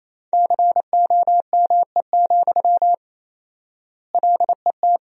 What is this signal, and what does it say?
Morse code, heard as a keyed tone, about 28 wpm, 700 Hz. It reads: COME, LET
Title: 4x Word-Spacing - Sentences from Top 500 Words - 28wpm